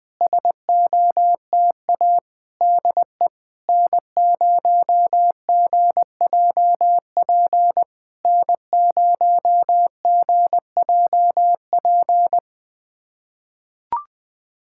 Morse code, heard as a keyed tone, about 20 wpm, 690 Hz. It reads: SOTA DE N0GJP N0GJP
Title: SOTA QSO Practice 20wpm